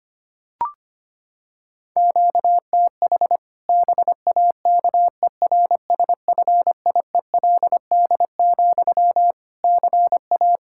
Morse code, heard as a keyed tone, about 25 wpm, 700 Hz. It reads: QTH BAKERSFIELD, CA
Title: Practice QSO - Exchange 1 - Beginning - 25wpm